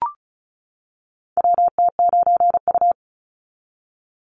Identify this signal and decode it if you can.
Morse code, heard as a keyed tone, about 36 wpm, 700 Hz. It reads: WT9U
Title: Call Signs - 35wpm